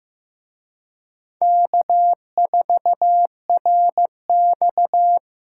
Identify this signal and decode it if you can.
Morse code, heard as a keyed tone, about 18 wpm, 700 Hz. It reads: K4RX
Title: Speed Racing - USA Call Signs 15wpm